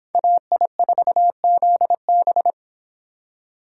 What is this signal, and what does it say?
Morse code, heard as a keyed tone, about 28 wpm, 700 Hz. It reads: AI4ZB